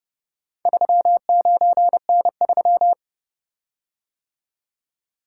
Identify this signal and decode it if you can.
Morse code, heard as a keyed tone, about 30 wpm, 700 Hz. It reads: 39N3